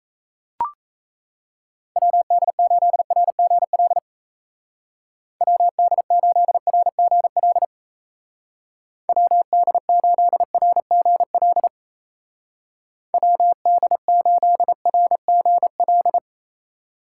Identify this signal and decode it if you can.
Morse code, heard as a keyed tone, about 33 wpm, 700 Hz. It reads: WD8RGL WD8RGL WD8RGL WD8RGL
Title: Speed Racing - USA 2x3 Call Signs 28wpm